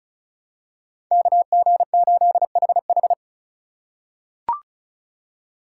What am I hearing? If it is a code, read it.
Morse code, heard as a keyed tone, about 34 wpm, 700 Hz. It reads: KG8HH
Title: USA Call Signs - 5 Characters - 3x Character Spacing - 35wpm